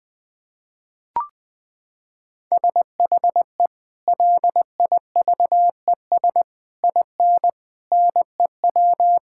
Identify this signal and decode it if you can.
Morse code, heard as a keyed tone, about 20 wpm, 700 Hz. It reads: SHE LIVES IN NEW
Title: Sentences with 3-Word Phrases - 20wpm